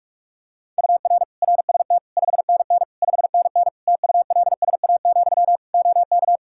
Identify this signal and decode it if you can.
Morse code, heard as a keyed tone, about 46 wpm, 700 Hz. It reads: UR RST 5NN 5NN TULSA, OK
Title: Practice QSO - Exchange 1 - Beginning - 45wpm